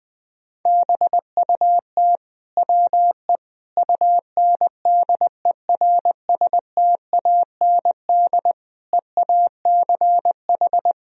Morse code, heard as a keyed tone, about 20 wpm, 700 Hz. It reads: BUT WE UNDERSTAND EACH
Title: Sentences from Top 1000 Words - Review Part 1 - 20wpm